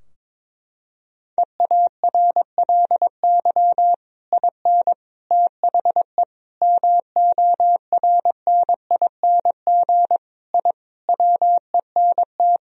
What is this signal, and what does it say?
Morse code, heard as a keyed tone, about 22 wpm, 700 Hz. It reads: EARLY IN THE MORNING I WENT
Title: Sentences from Top 1000 Words - Review Part 2 - 22wpm